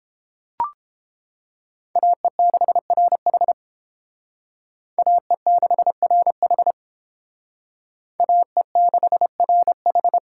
Morse code, heard as a keyed tone, about 28 wpm, 700 Hz. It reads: AE6RH AE6RH AE6RH